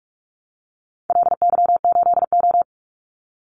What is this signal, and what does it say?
Morse code, heard as a keyed tone, about 44 wpm, 700 Hz. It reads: LY8O
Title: Call Signs - 45wpm